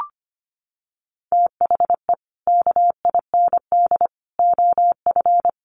Morse code, heard as a keyed tone, about 25 wpm, 700 Hz. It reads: THE KIND OF
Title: Sets of 5 Words Encore 25wpm